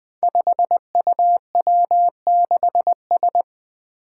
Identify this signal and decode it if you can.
Morse code, heard as a keyed tone, about 21 wpm, 700 Hz. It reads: 5UW6S
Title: Five Challenging Letter-Number Combinations 20wpm